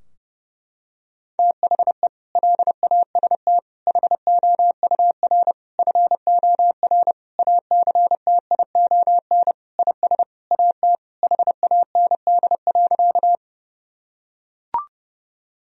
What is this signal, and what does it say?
Morse code, heard as a keyed tone, about 32 wpm, 700 Hz. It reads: THE LAST HOUR FOR ACTION IS AT HAND.
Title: Sentences from Top 1000 Words - Review Part 4 - 30wpm